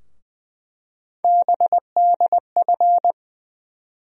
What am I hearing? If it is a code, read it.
Morse code, heard as a keyed tone, about 20 wpm, 700 Hz. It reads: BDF
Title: USA Call Sign Postfixes - 3 Letters 20wpm